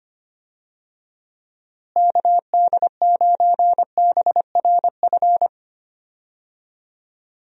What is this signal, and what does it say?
Morse code, heard as a keyed tone, about 25 wpm, 700 Hz. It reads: KD9BRF
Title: USA Call Signs - 6 Characters - Repeated 2x - 25wpm